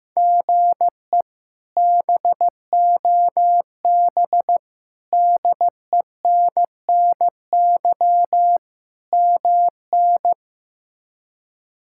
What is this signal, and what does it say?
Morse code, heard as a keyed tone, about 15 wpm, 700 Hz. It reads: GE BOB DENNY MN
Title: K1USN SST QSO Practice - 15wpm